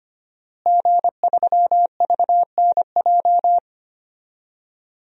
Morse code, heard as a keyed tone, about 25 wpm, 700 Hz. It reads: G3VNJ
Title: DXCC-100 Call Signs - 5 Characters - Repeated 3x - 25wpm